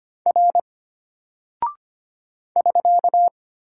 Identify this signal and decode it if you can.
Morse code, heard as a keyed tone, about 25 wpm, 700 Hz. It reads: R <SK>
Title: Single Character (Letter:Number:Punctuation:Prosign) Alt 25wpm